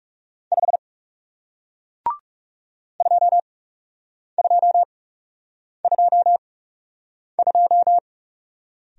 Morse code, heard as a keyed tone, about 37 wpm, spaced as for 8 wpm, 700 Hz. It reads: 5 2222
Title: Speed-Racing Single Letter-Number 30wpm